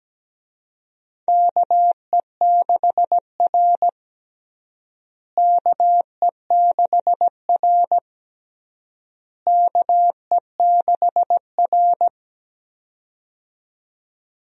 Morse code, heard as a keyed tone, about 17 wpm, 700 Hz. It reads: KE6R KE6R KE6R
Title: USA Call Signs - 4 Characters - Repeated 3x - 17wpm